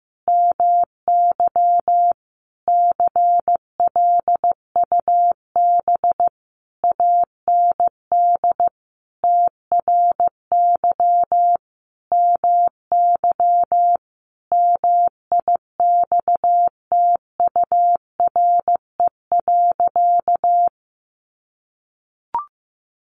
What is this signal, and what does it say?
Morse code, heard as a keyed tone, about 15 wpm, 700 Hz. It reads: MY CLUB AND TRY MY MIXTURE.
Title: Sentences from Top 1500 Words 15wpm